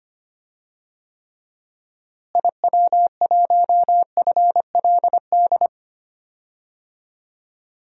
Morse code, heard as a keyed tone, about 25 wpm, 700 Hz. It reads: IW1FLD
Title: DXCC-100 Call Signs - 6 Characters - Repeated 3x - 25wpm